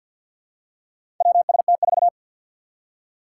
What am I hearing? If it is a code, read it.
Morse code, heard as a keyed tone, about 48 wpm, 700 Hz. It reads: WST4
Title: Four Challenging Letter-Number Combinations 50wpm